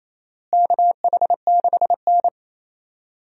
Morse code, heard as a keyed tone, about 28 wpm, 700 Hz. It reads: KH6N
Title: DXCC-100 Call Signs - 4 Characters - Repeated 2x - 28wpm